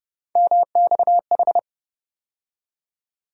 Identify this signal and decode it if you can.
Morse code, heard as a keyed tone, about 30 wpm, 700 Hz. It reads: MXH
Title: USA Call Sign Postfixes - 3 Letters 30wpm